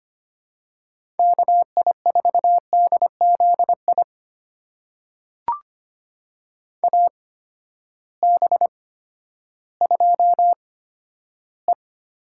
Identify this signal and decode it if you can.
Morse code, heard as a keyed tone, about 25 wpm, spaced as for 12 wpm, 700 Hz. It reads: KI4DZI AB2E
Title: USA Call Signs - 8x Character Spacing - 25wpm